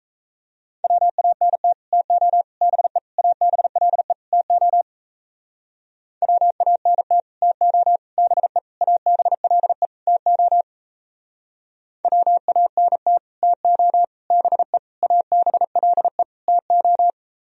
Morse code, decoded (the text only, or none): WANT TO BE ABLE TO WANT TO BE ABLE TO WANT TO BE ABLE TO